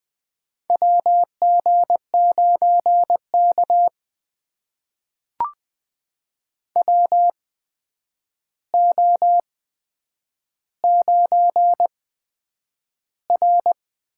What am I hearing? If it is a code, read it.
Morse code, heard as a keyed tone, about 20 wpm, spaced as for 9 wpm, 700 Hz. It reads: WG9K WO9R
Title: USA Call Signs - 4 Characters - 8x Character Spacing - 20wpm